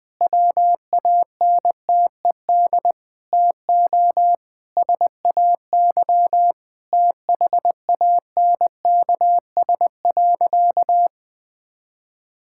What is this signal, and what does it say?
Morse code, heard as a keyed tone, about 21 wpm, 700 Hz. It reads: WANTED TO SAY THANKS.
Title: Sentences with 4-Word Phrases - 20wpm